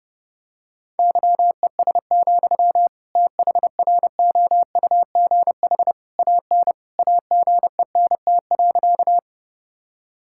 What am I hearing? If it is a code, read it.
Morse code, heard as a keyed tone, about 30 wpm, 700 Hz. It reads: YES, THROUGH AN AGENT.
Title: Sentences from Top 1600 Words 30wpm